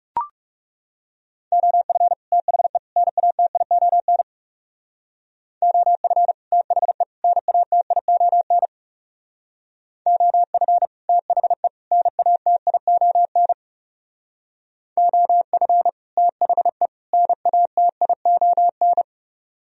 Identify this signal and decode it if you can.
Morse code, heard as a keyed tone, about 38 wpm, 690 Hz. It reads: OF THE NATION OF THE NATION OF THE NATION OF THE NATION